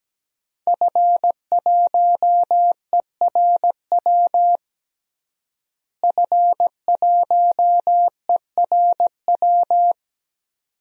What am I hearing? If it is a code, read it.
Morse code, heard as a keyed tone, about 17 wpm, 700 Hz. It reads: F1ERW F1ERW
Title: DXCC-100 Call Signs - 5 Characters - Repeated 2x - 17wpm